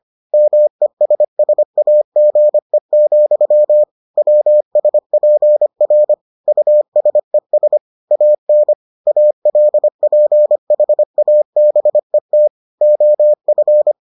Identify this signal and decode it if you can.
Morse code, heard as a keyed tone, about 25 wpm, 600 Hz. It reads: MESSAGE, WSPR USES AN ALPHABET OF